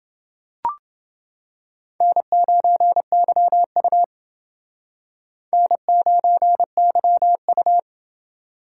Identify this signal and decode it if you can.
Morse code, heard as a keyed tone, about 29 wpm, 700 Hz. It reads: N9YU N9YU